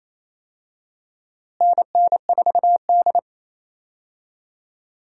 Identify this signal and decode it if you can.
Morse code, heard as a keyed tone, about 28 wpm, 700 Hz. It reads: NN4D
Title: USA Call Signs - 4 Characters - Repeated 3x - 28wpm